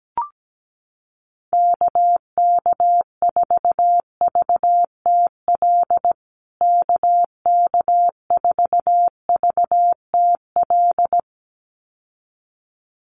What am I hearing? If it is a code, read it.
Morse code, heard as a keyed tone, about 17 wpm, 700 Hz. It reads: KK4VTL KK4VTL
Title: USA Call Signs Repeated 2x 17wpm